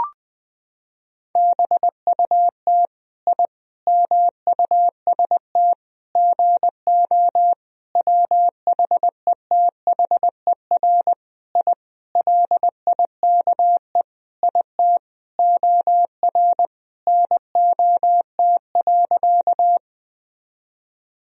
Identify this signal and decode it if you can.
Morse code, heard as a keyed tone, about 20 wpm, 700 Hz. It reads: BUT I MUST GO WHETHER I LIKE IT OR NOT.
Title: Sentences from Top 1000 Words - Review Part 3 - 20wpm